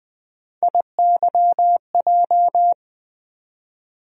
Morse code, heard as a keyed tone, about 20 wpm, 700 Hz. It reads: IYJ
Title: USA Call Sign Postfixes - 3 Letters 20wpm